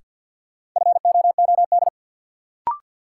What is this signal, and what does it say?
Morse code, heard as a keyed tone, about 53 wpm, 700 Hz. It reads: FOOD
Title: Sentences from Top 300 Words - New Word Drill - 50wpm